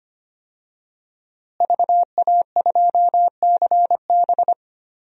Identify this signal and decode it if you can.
Morse code, heard as a keyed tone, about 25 wpm, 700 Hz. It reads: VA2CB
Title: DXCC-100 Call Signs - 5 Characters - Repeated 2x - 25wpm